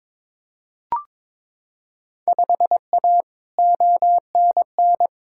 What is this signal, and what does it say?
Morse code, heard as a keyed tone, about 22 wpm, 700 Hz. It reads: 5A ONN
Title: Field Day Exchanges Encore - 22wpm